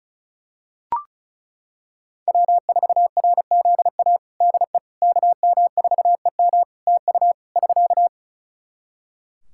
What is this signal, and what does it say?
Morse code, heard as a keyed tone, about 36 wpm, 700 Hz. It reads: W4RZA DE KM4EM TU <SK>
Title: Practice QSO - Exchange 4 - Ending - 35wpm